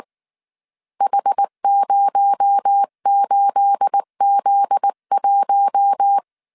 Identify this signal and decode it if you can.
Morse code, heard as a keyed tone, about 19 wpm, 800 Hz. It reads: H08Z1
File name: CW / DET / 1 9 1500C S B 18wpm 2551